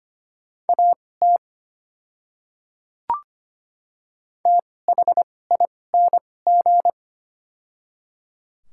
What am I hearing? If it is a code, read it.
Morse code, heard as a keyed tone, about 25 wpm, spaced as for 18 wpm, 700 Hz. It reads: AT THING
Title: 2x Character Spacing - Top 100 Words - 25wpm